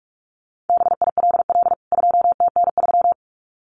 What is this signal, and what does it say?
Morse code, heard as a keyed tone, about 44 wpm, 700 Hz. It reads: BILL 2TN3
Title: CWT Exchange 45wpm